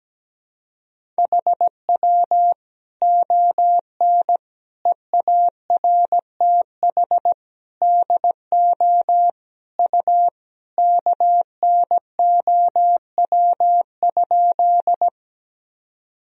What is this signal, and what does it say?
Morse code, heard as a keyed tone, about 17 wpm, 700 Hz. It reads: HW ON EARTH DO U KNOW?
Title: Sentences from Top 200 Words and Abbreviations 17wpm